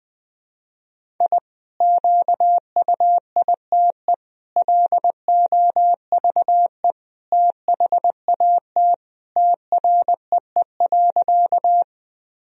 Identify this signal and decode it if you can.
Morse code, heard as a keyed tone, about 20 wpm, 700 Hz. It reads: I QUITE LOVE THAT TREE.
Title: Sentences from Top 1000 Words - Review Part 1 - 20wpm